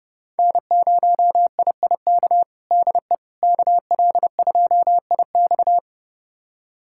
Morse code, heard as a keyed tone, about 30 wpm, 700 Hz. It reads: N0IIK DE KL2IX
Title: USA Callsign DE Callsign - 5 Characters - 30wpm